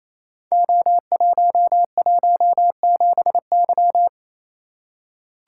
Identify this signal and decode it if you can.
Morse code, heard as a keyed tone, about 28 wpm, 700 Hz. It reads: O117Y
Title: Code Groups - 5 Letter:Numbers - 28wpm